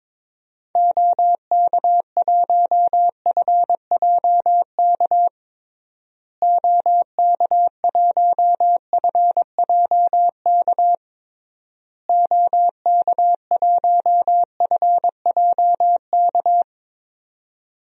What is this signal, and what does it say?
Morse code, heard as a keyed tone, about 22 wpm, 700 Hz. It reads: OK1FJK OK1FJK OK1FJK
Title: DXCC-100 Call Signs - 6 Characters - Repeated 3x - 22wpm